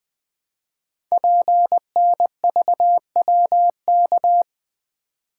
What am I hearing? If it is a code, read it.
Morse code, heard as a keyed tone, about 20 wpm, 700 Hz. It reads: PNVWK
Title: Code Groups - 5 Letters - 20wpm